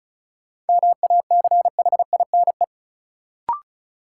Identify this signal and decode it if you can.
Morse code, heard as a keyed tone, about 36 wpm, 700 Hz. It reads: MACHINE
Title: Sentences from Top 1000 Words - Review Part 2 - New Word Drill - 35wpm